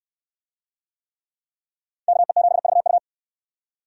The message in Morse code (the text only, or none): KE7VU